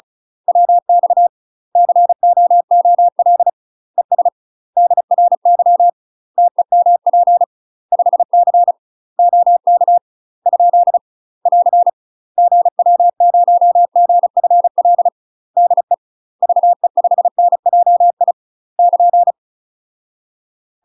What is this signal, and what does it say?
Morse code, heard as a keyed tone, about 36 wpm, 700 Hz. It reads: WX COOL ES DRY TEMP 5C OK ? <AR> GW0GFL DE VE5NJI <KN>
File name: UK MASTER QSOs TRAINING / 35WPM / UK MASTER QSOs 0007